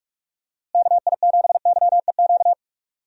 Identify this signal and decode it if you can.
Morse code, heard as a keyed tone, about 42 wpm, 700 Hz. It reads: KI7YEQ
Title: USA Call Signs - 6 Characters - Repeated 3x - 45wpm